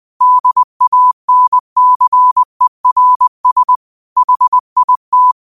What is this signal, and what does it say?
Morse code, heard as a keyed tone, about 20 wpm, 1000 Hz. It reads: DANCERS HIT
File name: 20wpm